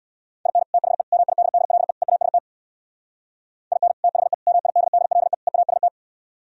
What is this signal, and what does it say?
Morse code, heard as a keyed tone, about 62 wpm, 700 Hz. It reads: IN THE MIDDLE EAST IN THE MIDDLE EAST
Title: Speed-Racing - Sets of 4 Words 50wpm